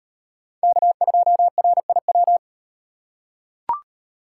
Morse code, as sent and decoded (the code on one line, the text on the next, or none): -.- ..--- .-. .. .--
K2RIW